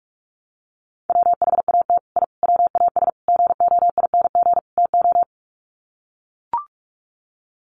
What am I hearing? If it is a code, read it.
Morse code, heard as a keyed tone, about 44 wpm, 700 Hz. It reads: WHAT I WAS GOING TO
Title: Sets of 5 Words Encore2 45wpm